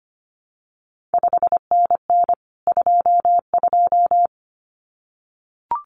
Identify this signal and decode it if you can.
Morse code, heard as a keyed tone, about 25 wpm, 700 Hz. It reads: 5NN 22
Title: CQ WW DX Contest Exchange Encore - 25wpm